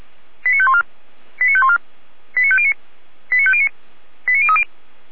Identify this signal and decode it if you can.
Five-tone selective call (ZVEI), heard as quick runs of five tones, tones of about 70 ms. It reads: 89514 89514 89509 89509 89030